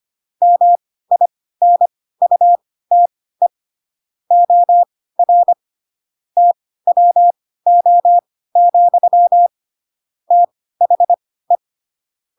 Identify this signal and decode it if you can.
Morse code, heard as a keyed tone, about 25 wpm, spaced as for 16 wpm, 700 Hz. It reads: MINUTE OR TWO, THE
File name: Pinocchio / Pinocchio ch6 10 25 ik1hge